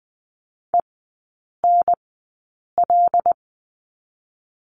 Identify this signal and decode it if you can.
Morse code, heard as a keyed tone, about 20 wpm, spaced as for 8 wpm, 700 Hz. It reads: ENL
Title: 121 Lesson 20e - Number 3 - Three Characters 20wpm